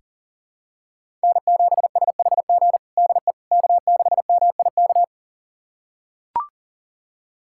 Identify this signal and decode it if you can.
Morse code, heard as a keyed tone, about 40 wpm, 700 Hz. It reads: N7SHG DE K6MIK
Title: USA Callsign DE Callsign - 5 Characters - 40wpm